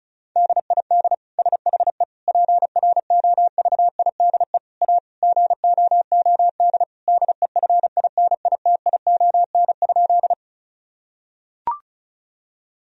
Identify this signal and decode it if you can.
Morse code, heard as a keyed tone, about 36 wpm, 700 Hz. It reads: DID SHE PROVIDE A GOOD DEFINITION?
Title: Sentences from Top 1000 Words - Review Part 4 - 35wpm